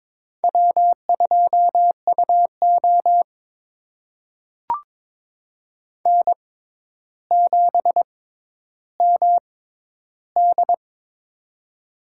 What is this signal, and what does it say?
Morse code, heard as a keyed tone, about 22 wpm, spaced as for 11 wpm, 700 Hz. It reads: W2UO N7MD
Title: USA Call Signs - 4 Characters - 6x Character Spacing - 22wpm